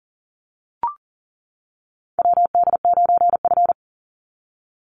Morse code, heard as a keyed tone, about 38 wpm, 700 Hz. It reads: WD9F
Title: NA SOTA Chaser Callsigns 40wpm